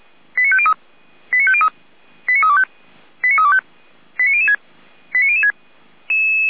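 Five-tone selective call (ZVEI), heard as quick runs of five tones, tones of about 70 ms. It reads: 89503 89503 89325 89325 89006 89006
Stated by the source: ZVEI five-tone selective call received on an SDR.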